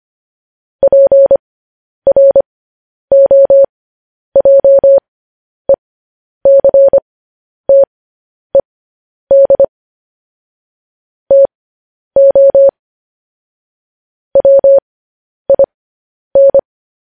Morse code, heard as a keyed tone, about 25 wpm, spaced as for 10 wpm, 550 Hz. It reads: PROJECTED TO WIN